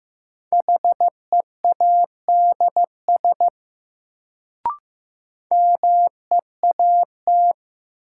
Morse code, heard as a keyed tone, about 15 wpm, 700 Hz. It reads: HEADS MEAT
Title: Sentences from Top 1000 Words - Review Part 3 - New Word Drill - 15wpm